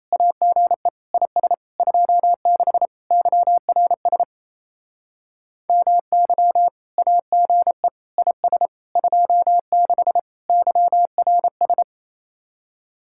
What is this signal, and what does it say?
Morse code, heard as a keyed tone, about 34 wpm, 700 Hz. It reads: AGE IS 26 YRS MY AGE IS 26 YRS
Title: Speed-Racing Common QSO Phrases - 28wpm